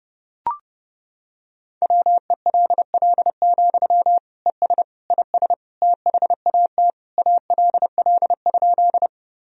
Morse code, heard as a keyed tone, about 30 wpm, 700 Hz. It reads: WELL, ES IS THAT ALL?
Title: Sentences from Top 200 Words and Abbreviations 30wpm